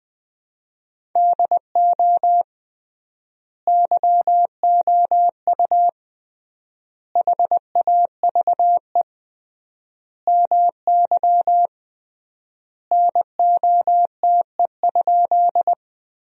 Morse code, heard as a keyed tone, about 20 wpm, 700 Hz. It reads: DO YOU HAVE MY NOTE?
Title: 3x Word-Spacing - Sentences from Top 500 Words - 20wpm